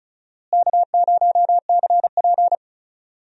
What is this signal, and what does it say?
Morse code, heard as a keyed tone, about 36 wpm, 700 Hz. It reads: K0CP
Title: USA Call Signs - 4 Characters - Repeated 2x - 35wpm